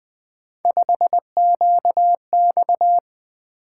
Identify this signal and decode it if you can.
Morse code, heard as a keyed tone, about 20 wpm, 700 Hz. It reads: 5QX
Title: Three Letter-Number Combinations - 20wpm